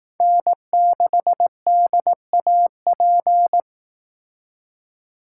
Morse code, heard as a keyed tone, about 18 wpm, 700 Hz. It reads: N6DAP
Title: Speed Racing - USA 1x3 Call Signs 15wpm